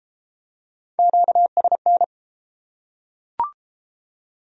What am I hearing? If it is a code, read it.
Morse code, heard as a keyed tone, about 31 wpm, 700 Hz. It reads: QSN